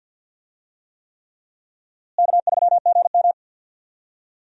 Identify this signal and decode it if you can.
Morse code, heard as a keyed tone, about 46 wpm, 700 Hz. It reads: K3GM
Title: USA Call Signs - 4 Characters - Repeated 3x - 50wpm